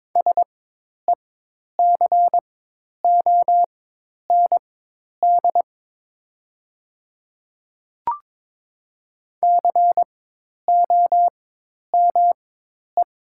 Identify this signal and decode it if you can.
Morse code, heard as a keyed tone, about 22 wpm, spaced as for 10 wpm, 700 Hz. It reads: SECOND COME
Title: 4x Character Spacing - Top 300 Words - 22wpm